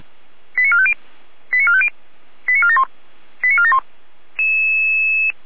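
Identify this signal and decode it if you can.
Five-tone selective call (ZVEI), heard as quick runs of five tones, tones of about 70 ms. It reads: 89450 89450 89471 89471
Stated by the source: ZVEI five-tone selective call received on an SDR.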